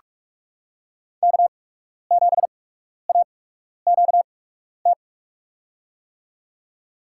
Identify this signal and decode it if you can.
Morse code, heard as a keyed tone, about 47 wpm, spaced as for 12 wpm, 700 Hz. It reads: K7AQT
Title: USA Call Signs - 8x Character Spacing - 45wpm